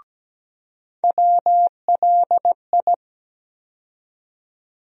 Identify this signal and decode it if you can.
Morse code, heard as a keyed tone, about 17 wpm, 700 Hz. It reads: WLI
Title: USA Call Sign Postfixes - 3 Letters 17wpm